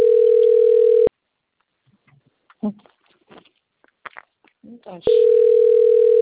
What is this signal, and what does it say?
Ringback tone (North America), 440 + 480 Hz, with speech.
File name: rg-1600-unknown- - - .54273